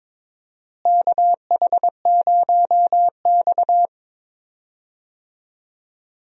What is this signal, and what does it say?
Morse code, heard as a keyed tone, about 21 wpm, 700 Hz. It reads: KH0X
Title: USA Call Signs - 4 Characters - Repeated 2x - 22wpm